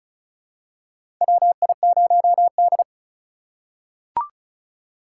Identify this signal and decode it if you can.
Morse code, heard as a keyed tone, about 34 wpm, 700 Hz. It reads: WI0D
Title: USA Call Signs - 4 Characters - 6x Character Spacing - 35wpm